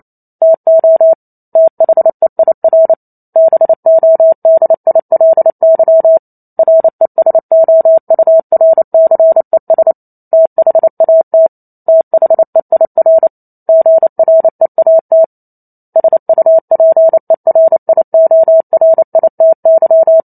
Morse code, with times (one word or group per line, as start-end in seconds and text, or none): 0.4-1.1 TO
1.6-2.9 THEIR
3.4-6.2 BODILY
6.6-9.9 RESOURCES
10.3-11.5 THAT
11.9-13.3 THEIR
13.7-15.2 GREAT
16.0-20.2 SUPERIORITY